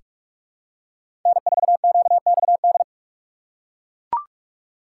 Morse code, heard as a keyed tone, about 42 wpm, 700 Hz. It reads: N4QXD
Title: USA Call Signs - 5 Characters - 2x Character Spacing - 45wpm